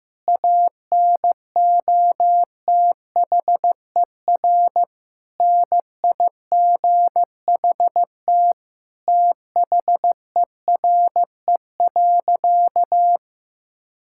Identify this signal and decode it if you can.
Morse code, heard as a keyed tone, about 15 wpm, 700 Hz. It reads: ANOTHER NIGHT THERE.
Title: Sentences from Top 1000 Words - Review Part 5 - 15wpm